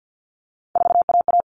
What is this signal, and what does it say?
Morse code, heard as a keyed tone, about 50 wpm, 700 Hz. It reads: 4AA
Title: All Cut Numbers - 3-Digit 50wpm